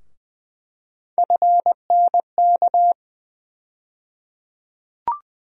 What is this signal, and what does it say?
Morse code, heard as a keyed tone, about 20 wpm, 700 Hz. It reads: FNK
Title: USA Call Sign Postfixes - 3 Letters 20wpm